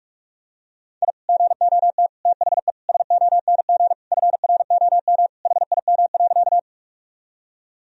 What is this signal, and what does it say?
Morse code, heard as a keyed tone, about 46 wpm, 700 Hz. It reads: I GOT THE SONG FROM HIM.
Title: Sentences from Top 1000 Words - Review Part 2 - 45wpm